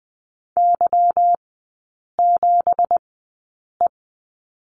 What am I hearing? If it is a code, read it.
Morse code, heard as a keyed tone, about 20 wpm, spaced as for 8 wpm, 700 Hz. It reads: Y7E
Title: 166 Lesson 27e - Number 7 - Three Characters 20wpm